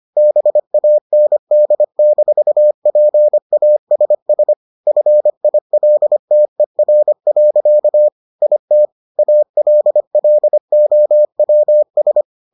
Morse code, heard as a keyed tone, about 25 wpm, 600 Hz. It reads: BAND-PASS FILTER. IT ALLOWS